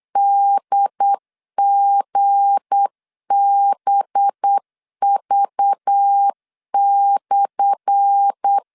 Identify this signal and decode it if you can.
Morse code, heard as a keyed tone, about 8 wpm, 800 Hz. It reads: DGBV/
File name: LEZIONI / 10 SIGNS / 10A SLASH 8wpm 3452